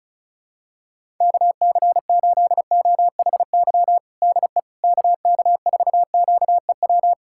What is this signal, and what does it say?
Morse code, heard as a keyed tone, about 36 wpm, 700 Hz. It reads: KC8OHY DE KK4QEW
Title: USA Callsign DE Callsign 35wpm